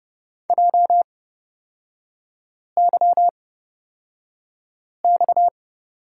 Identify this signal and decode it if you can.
Morse code, heard as a keyed tone, about 30 wpm, spaced as for 5 wpm, 700 Hz. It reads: JYX
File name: Lesson 37e - Letter x - Three Characters 30wpm